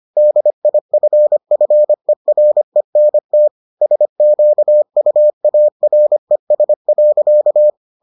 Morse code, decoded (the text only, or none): DIFFERENT SQUARES.